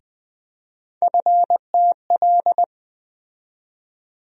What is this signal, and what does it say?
Morse code, heard as a keyed tone, about 20 wpm, 700 Hz. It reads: FTL
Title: USA Call Sign Postfixes - 3 Letters 20wpm